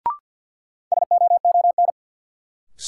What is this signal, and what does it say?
Morse code, heard as a keyed tone, about 53 wpm, 700 Hz. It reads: SOON
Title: Sentences from Top 400 Words - New Word Drill - 50wpm